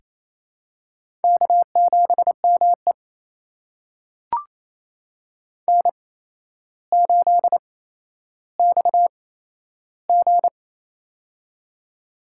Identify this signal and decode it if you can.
Morse code, heard as a keyed tone, about 28 wpm, spaced as for 12 wpm, 700 Hz. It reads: K7ME N8XG
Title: USA Call Signs - 4 Characters - 8x Character Spacing - 28wpm